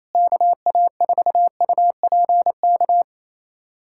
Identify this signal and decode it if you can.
Morse code, heard as a keyed tone, about 28 wpm, 700 Hz. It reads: KA4UPK
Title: USA Callsign DE Callsign 28wpm